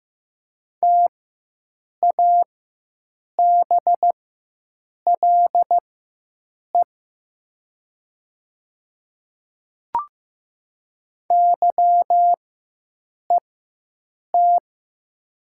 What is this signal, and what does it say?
Morse code, heard as a keyed tone, about 15 wpm, spaced as for 7 wpm, 700 Hz. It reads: TABLE YET
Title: 4x Character Spacing - Top 500 Words - 15wpm